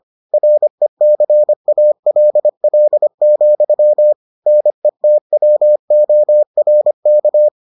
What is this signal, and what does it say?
Morse code, heard as a keyed tone, about 25 wpm, 600 Hz. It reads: RECALL, NETWORK